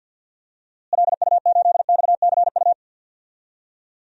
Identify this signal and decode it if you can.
Morse code, heard as a keyed tone, about 51 wpm, 700 Hz. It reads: LU8XCU
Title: DXCC-100 Call Signs - 6 Characters - Repeated 2x - 50wpm